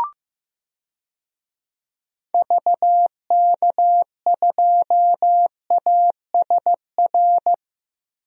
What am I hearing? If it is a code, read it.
Morse code, heard as a keyed tone, about 16 wpm, 700 Hz. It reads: VK2ASR
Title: DXCC-100 Call Signs - 6 Characters - Repeated 2x - 15wpm